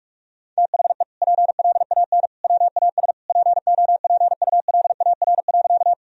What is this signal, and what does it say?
Morse code, heard as a keyed tone, about 46 wpm, 700 Hz. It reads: THE PLAN WAS POPULAR.
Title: Sentences from Top 1000 Words - Review Part 3 - 45wpm